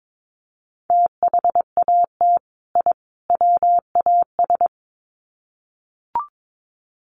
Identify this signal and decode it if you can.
Morse code, heard as a keyed tone, about 22 wpm, 700 Hz. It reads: THAT I WAS
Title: Sets of 3 Words 22wpm